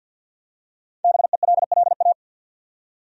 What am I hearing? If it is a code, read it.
Morse code, heard as a keyed tone, about 52 wpm, 700 Hz. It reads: BELLA
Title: USA Common Names - 50wpm